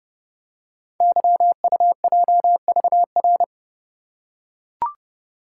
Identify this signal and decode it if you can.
Morse code, heard as a keyed tone, about 30 wpm, 700 Hz. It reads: YUJVR
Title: Code Groups - 5 Letters - 30wpm